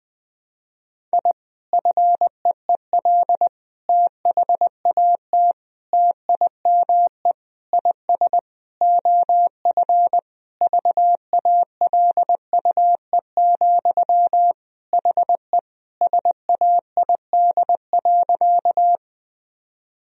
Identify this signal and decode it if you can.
Morse code, heard as a keyed tone, about 20 wpm, 700 Hz. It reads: I FEEL THAT TIME IS OF VALUE, HE SAID.
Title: Sentences from Top 1000 Words - Review Part 3 - 20wpm